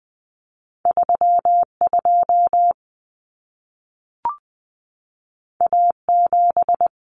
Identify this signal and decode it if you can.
Morse code, heard as a keyed tone, about 20 wpm, 700 Hz. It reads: 32 A7
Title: Common Cut Numbers - 2-Digit 20wpm